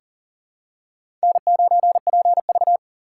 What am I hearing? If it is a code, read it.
Morse code, heard as a keyed tone, about 37 wpm, 700 Hz. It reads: N9PV
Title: USA Call Signs - 2x Character Spacing - 40wpm